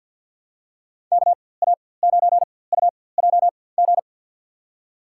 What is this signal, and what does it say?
Morse code, heard as a keyed tone, about 53 wpm, spaced as for 24 wpm, 700 Hz. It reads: KA9UJG
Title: USA Call Signs - 6 Characters - 4x Character Spacing - 50wpm